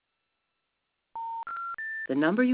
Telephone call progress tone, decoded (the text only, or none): special information tone